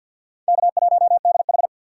Morse code, heard as a keyed tone, about 53 wpm, 700 Hz. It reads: K1DH
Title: USA Call Signs - 4 Characters - 6x Character Spacing - 50wpm